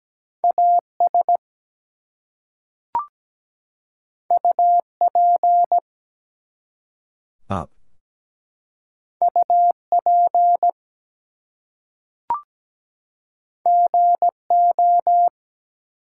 Morse code, heard as a keyed tone, about 17 wpm, 700 Hz. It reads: AS UP UP GO
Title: Two Letter Words - 17wpm